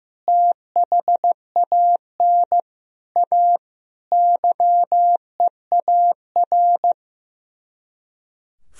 Morse code, heard as a keyed tone, about 15 wpm, 700 Hz. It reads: THAN A YEAR
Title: Speed-Racing - Sets of 5 Words 15wpm